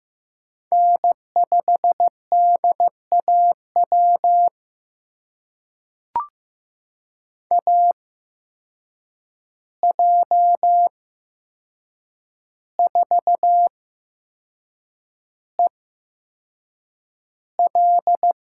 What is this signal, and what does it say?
Morse code, heard as a keyed tone, about 15 wpm, spaced as for 6 wpm, 700 Hz. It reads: N5DAW AJ4EL
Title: USA Call Signs - 5 Characters - 8x Character Spacing - 15wpm